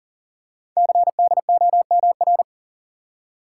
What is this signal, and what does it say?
Morse code, heard as a keyed tone, about 41 wpm, 700 Hz. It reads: CDOMR
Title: Code Groups - 5 Letters - 40wpm